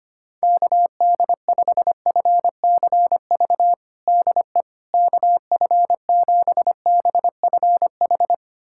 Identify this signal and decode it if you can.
Morse code, heard as a keyed tone, about 25 wpm, 700 Hz. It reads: KD5FCV DE KF7BFH
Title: USA Callsign DE Callsign 25wpm